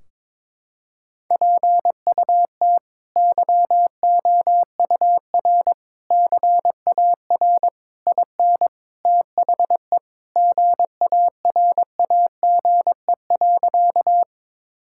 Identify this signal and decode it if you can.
Morse code, heard as a keyed tone, about 22 wpm, 700 Hz. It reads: PUT YOUR CAR IN THE GARAGE.
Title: Sentences from Top 1600 Words 22wpm